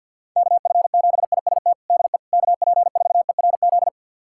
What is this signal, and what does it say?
Morse code, heard as a keyed tone, about 51 wpm, 690 Hz. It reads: KF7IST DE KP4ERZ
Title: USA Callsign DE Callsign - 6 Characters - 50wpm